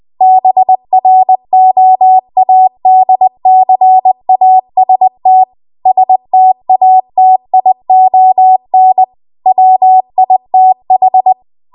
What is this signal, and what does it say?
Morse code, heard as a keyed tone, about 20 wpm, 750 Hz. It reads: BROADCAST STATION WITH
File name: Archive / 20 WPM / 20WPM